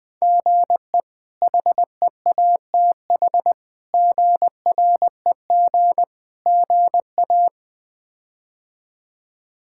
Morse code, heard as a keyed tone, about 20 wpm, 700 Hz. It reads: GE HEATH GREG GA
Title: K1USN SST QSO Practice - 20wpm